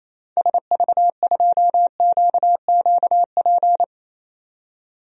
Morse code, heard as a keyed tone, about 28 wpm, 700 Hz. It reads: SV2QQP
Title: DXCC-100 Call Signs - 6 Characters - 28wpm